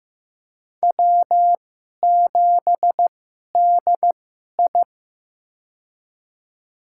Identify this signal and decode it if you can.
Morse code, heard as a keyed tone, about 15 wpm, spaced as for 11 wpm, 700 Hz. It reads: W7DI
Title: USA Call Signs - 2x Character Spacing - 15wpm